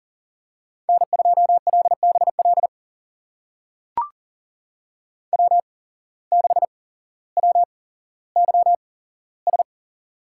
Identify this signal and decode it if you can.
Morse code, heard as a keyed tone, about 40 wpm, spaced as for 17 wpm, 700 Hz. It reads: N2LBL W6WYS
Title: USA Call Signs - 8x Character Spacing - 40wpm